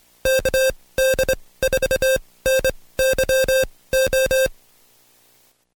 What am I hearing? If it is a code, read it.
Morse code, heard as a keyed tone, about 25 wpm, spaced as for 18 wpm, 530 Hz. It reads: KD4NYO
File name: Code Quick / AudioDownld2 / Seg136-Stepper9 19WPM